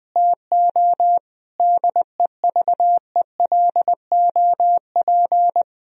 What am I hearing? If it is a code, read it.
Morse code, heard as a keyed tone, about 20 wpm, 700 Hz. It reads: TO DEVELOP
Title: Sentences from Top 1000 Words - Review Part 2 - 20wpm